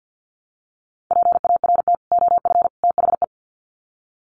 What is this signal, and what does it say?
Morse code, heard as a keyed tone, about 50 wpm, 700 Hz. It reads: PART OF THE
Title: Sets of 3 Words 50wpm